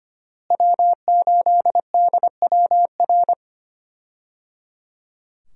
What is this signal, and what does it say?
Morse code, heard as a keyed tone, about 25 wpm, 700 Hz. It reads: W8DWR
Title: USA Call Signs - 5 Characters - Repeated 3x - 25wpm